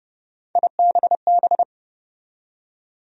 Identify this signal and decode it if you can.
Morse code, heard as a keyed tone, about 30 wpm, 700 Hz. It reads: IBB
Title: USA Call Sign Postfixes - 3 Letters 30wpm